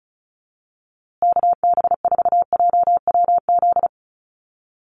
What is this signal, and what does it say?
Morse code, heard as a keyed tone, about 34 wpm, 700 Hz. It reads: KB4JWZ
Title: USA Call Signs Repeated 2x 35wpm